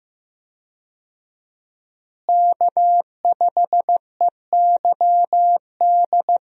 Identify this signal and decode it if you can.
Morse code, heard as a keyed tone, about 15 wpm, 700 Hz. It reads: K5EYD
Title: USA Call Signs - 5 Characters - Repeated 2x - 15wpm